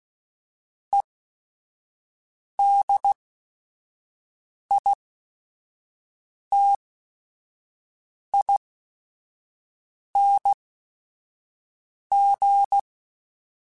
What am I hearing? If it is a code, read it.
Morse code, heard as a keyed tone, about 16 wpm, spaced as for 5 wpm, 800 Hz. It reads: EDITING